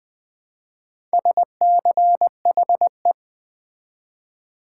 Morse code, heard as a keyed tone, about 20 wpm, 700 Hz. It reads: SCHE
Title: Build A Word - 2 Letters At A Time - 6-Letter Words - 20wpm